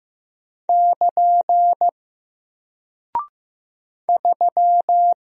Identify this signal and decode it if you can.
Morse code, heard as a keyed tone, about 15 wpm, 700 Hz. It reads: <KN> 3
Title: Single Character (Letter:Number:Punctuation:Prosign) 15wpm